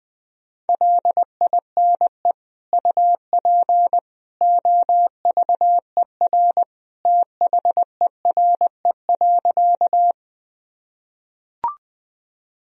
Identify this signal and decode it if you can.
Morse code, heard as a keyed tone, about 20 wpm, 700 Hz. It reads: LINE UP OVER THERE.
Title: Sentences from Top 1000 Words - Review Part 1 - 20wpm